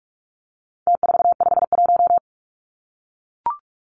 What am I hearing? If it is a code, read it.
Morse code, heard as a keyed tone, about 44 wpm, 700 Hz. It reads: T451
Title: Numbers - Contest Style - 4-Digit 45wpm